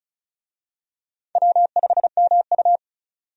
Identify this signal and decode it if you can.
Morse code, heard as a keyed tone, about 34 wpm, 700 Hz. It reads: W5MU
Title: USA Call Signs - 4 Characters - 35wpm